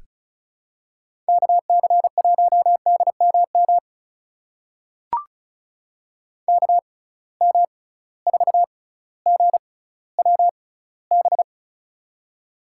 Morse code, heard as a keyed tone, about 36 wpm, spaced as for 18 wpm, 700 Hz. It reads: KC1DMM KM4GWB
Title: USA Call Signs - 6x Character Spacing - 35wpm